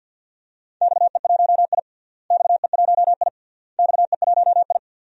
Morse code, heard as a keyed tone, about 52 wpm, 700 Hz. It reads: XE1I XE1I XE1I
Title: DXCC-100 Call Signs - 4 Characters - Repeated 3x - 50wpm